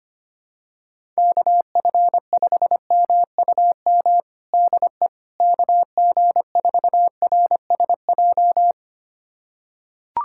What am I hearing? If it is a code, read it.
Morse code, heard as a keyed tone, about 25 wpm, 700 Hz. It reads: KF5MUM DE KG4RSJ
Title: USA Callsign DE Callsign - 6 Characters - 25wpm